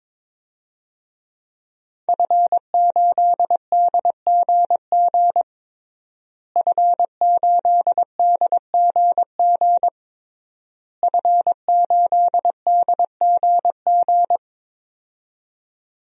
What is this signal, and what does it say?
Morse code, heard as a keyed tone, about 22 wpm, 700 Hz. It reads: F8DGG F8DGG F8DGG
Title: DXCC-100 Call Signs - 5 Characters - Repeated 3x - 22wpm